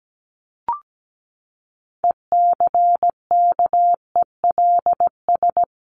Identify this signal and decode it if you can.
Morse code, heard as a keyed tone, about 17 wpm, 700 Hz. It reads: ECKELS
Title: A Sound Of Thunder 17wpm